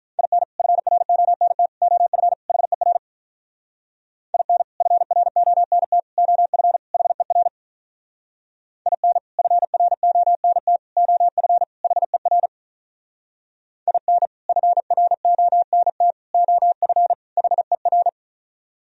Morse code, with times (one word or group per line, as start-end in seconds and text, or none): 0.2-0.4 IN
0.6-1.7 FRONT
1.8-2.3 OF
2.5-3.0 HER
4.3-4.6 IN
4.8-6.0 FRONT
6.2-6.8 OF
6.9-7.5 HER
8.9-9.2 IN
9.4-10.8 FRONT
11.0-11.6 OF
11.8-12.5 HER
13.9-14.3 IN
14.5-16.1 FRONT
16.3-17.1 OF
17.4-18.1 HER